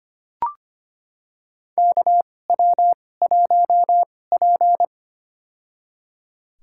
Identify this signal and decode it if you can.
Morse code, heard as a keyed tone, about 25 wpm, spaced as for 18 wpm, 700 Hz. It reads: KW1P
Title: USA Call Signs - 2x Character Spacing - 25wpm